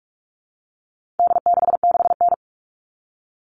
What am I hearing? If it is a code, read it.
Morse code, heard as a keyed tone, about 44 wpm, 700 Hz. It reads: D66N
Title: All Cut Numbers - 4-Digit 45wpm